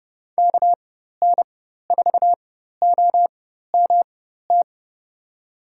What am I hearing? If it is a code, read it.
Morse code, heard as a keyed tone, about 30 wpm, spaced as for 14 wpm, 700 Hz. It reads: KN4OMT
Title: USA Call Signs - 6 Characters - 4x Character Spacing - 30wpm